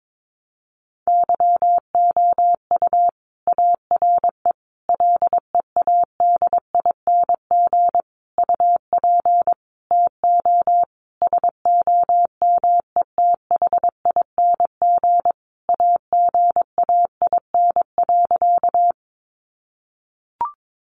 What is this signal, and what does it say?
Morse code, heard as a keyed tone, about 22 wpm, 700 Hz. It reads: YOU ARE LEADING UP TO SOMETHING AGAIN.
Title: Sentences from Top 1100 Words 22wpm